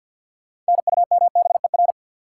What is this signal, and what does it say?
Morse code, heard as a keyed tone, about 54 wpm, 700 Hz. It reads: NUMBER
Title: Sentences from Top 100 Words - New Word Drill - 50wpm